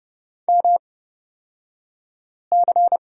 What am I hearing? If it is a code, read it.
Morse code, heard as a keyed tone, about 29 wpm, spaced as for 5 wpm, 700 Hz. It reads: MC
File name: Lesson 36d - BT - Two Characters 30wpm